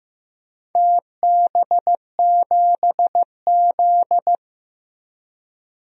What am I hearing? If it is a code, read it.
Morse code, heard as a keyed tone, about 16 wpm, 700 Hz. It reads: TB7Z
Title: DXCC-100 Call Signs - 4 Characters - 15wpm